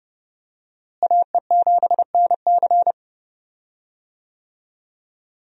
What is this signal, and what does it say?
Morse code, heard as a keyed tone, about 30 wpm, 700 Hz. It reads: AE7NC
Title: USA Call Signs - 5 Characters - Repeated 2x - 30wpm